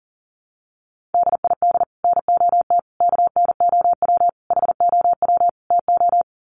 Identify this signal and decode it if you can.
Morse code, heard as a keyed tone, about 40 wpm, 700 Hz. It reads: DID NOT KNOW HOW TO
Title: Sets of 5 Words 40wpm